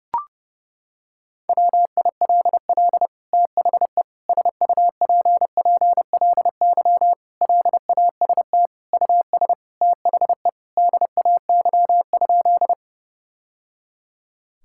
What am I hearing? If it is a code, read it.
Morse code, heard as a keyed tone, about 32 wpm, 700 Hz. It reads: WILL THE SUPPLY LAST US THE DAY?
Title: Sentences from Top 1000 Words - Review Part 5 - 30wpm